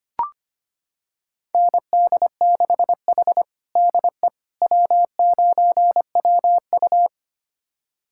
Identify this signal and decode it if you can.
Morse code, heard as a keyed tone, about 25 wpm, 700 Hz. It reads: ND6H DE W9WU
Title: USA Callsign DE Callsign - 4 Characters - 25wpm